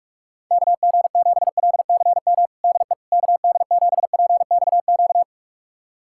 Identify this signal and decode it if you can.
Morse code, heard as a keyed tone, about 46 wpm, 700 Hz. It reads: KG7LCM DE KD7PXQ
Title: USA Callsign DE Callsign - 6 Characters - 45wpm